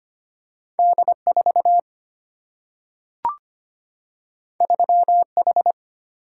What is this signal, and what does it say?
Morse code, heard as a keyed tone, about 25 wpm, 700 Hz. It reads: D 4 3 H
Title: Single-Letter Single-Number 25wpm